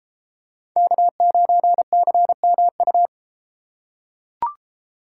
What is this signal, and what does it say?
Morse code, heard as a keyed tone, about 32 wpm, 700 Hz. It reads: K9CMU